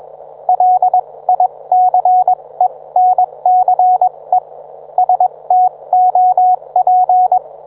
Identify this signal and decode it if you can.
Morse code, heard as a keyed tone, about 21 wpm, spaced as for 18 wpm, 730 Hz, with radio noise. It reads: LICENCE STOP
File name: book